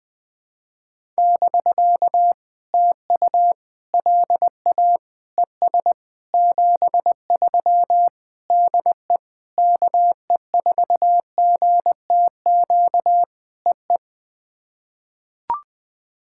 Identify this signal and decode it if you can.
Morse code, heard as a keyed tone, about 20 wpm, 700 Hz. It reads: <BK> TU LA ES 73 DE KE4GTQ EE
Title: POTA QSO Practice - 20wpm